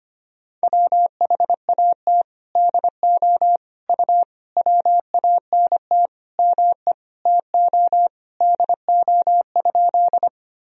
Morse code, heard as a keyed tone, about 25 wpm, 700 Hz. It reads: WHAT DO U WANT ME TO DO?
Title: Sentences from Top 200 Words and Abbreviations 25wpm